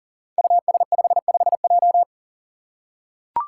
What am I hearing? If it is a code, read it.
Morse code, heard as a keyed tone, about 39 wpm, 700 Hz. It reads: US55J
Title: Five Challenging Letter-Number Combinations 40wpm